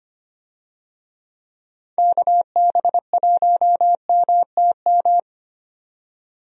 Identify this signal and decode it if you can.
Morse code, heard as a keyed tone, about 25 wpm, 700 Hz. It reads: KB1MTM
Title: USA Call Signs - 6 Characters - 25wpm